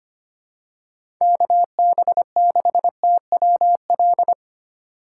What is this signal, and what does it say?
Morse code, heard as a keyed tone, about 25 wpm, 700 Hz. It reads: KB6TWL
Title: USA Call Signs - 6 Characters - 6x Character Spacing - 25wpm